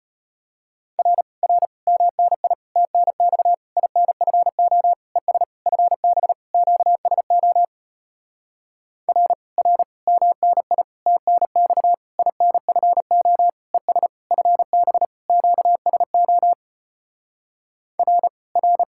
Morse code, heard as a keyed tone, about 33 wpm, 700 Hz. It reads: R R MNI TNX INFO ES FB QSO R R MNI TNX INFO ES FB QSO R R